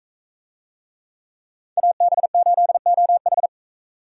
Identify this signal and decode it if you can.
Morse code, heard as a keyed tone, about 38 wpm, 700 Hz. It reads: AB8OH